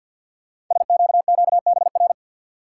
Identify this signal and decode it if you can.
Morse code, heard as a keyed tone, about 50 wpm, 690 Hz. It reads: SQQBR
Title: Code Groups - 5 Letters - 50wpm